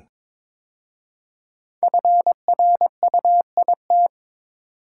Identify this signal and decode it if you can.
Morse code, heard as a keyed tone, about 22 wpm, 700 Hz. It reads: FRUIT
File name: 5 Letter Words - First 3 Letters Given - 22wpm